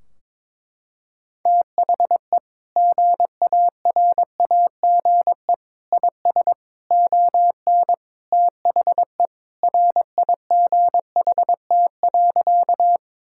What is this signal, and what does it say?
Morse code, heard as a keyed tone, about 22 wpm, 700 Hz. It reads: THE GARAGE IS ON THE RIGHT.
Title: Sentences from Top 1600 Words 22wpm